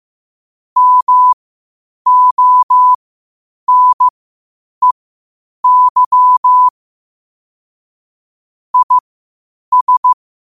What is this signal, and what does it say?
Morse code, heard as a keyed tone, about 15 wpm, spaced as for 8 wpm, 1000 Hz. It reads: MONEY IS
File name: qotdfiles / 8wpm